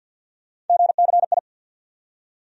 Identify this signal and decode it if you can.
Morse code, heard as a keyed tone, about 52 wpm, 700 Hz. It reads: GCI
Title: USA Call Sign Postfixes - 3 Letters 50wpm